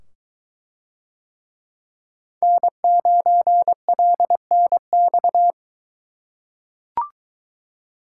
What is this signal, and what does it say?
Morse code, heard as a keyed tone, about 21 wpm, 700 Hz. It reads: N9LNX